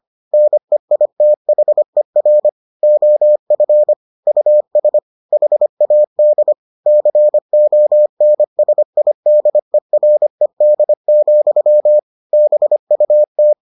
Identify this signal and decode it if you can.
Morse code, heard as a keyed tone, about 25 wpm, 600 Hz. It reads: NEITHER OF US HAD CONSIDERED, BUT